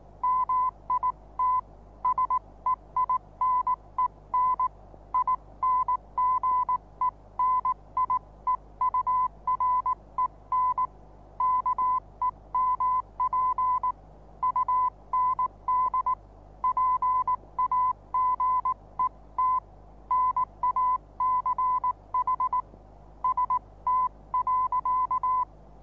Morse code, heard as a keyed tone, about 19 wpm, 1010 Hz. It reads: MIT SEINEN INGENIEUREN KEMP UND PAGET NACH ST.
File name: repos / Morsekurs / DJ4UF Morsen26